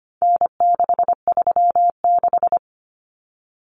Morse code, heard as a keyed tone, about 25 wpm, 700 Hz. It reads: N636
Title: Common Cut Numbers - 4-Digit 25wpm